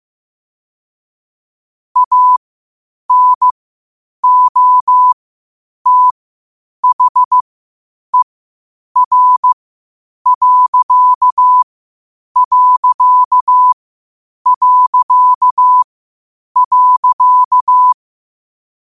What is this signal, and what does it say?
Morse code, heard as a keyed tone, about 15 wpm, spaced as for 8 wpm, 1000 Hz. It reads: ANOTHER....
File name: qotdfiles / 8wpm